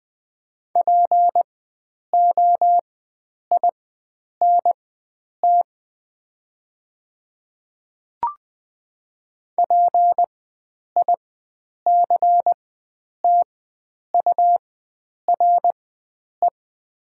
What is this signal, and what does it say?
Morse code, heard as a keyed tone, about 20 wpm, spaced as for 9 wpm, 700 Hz. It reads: POINT PICTURE
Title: 4x Character Spacing - Top 300 Words - 20wpm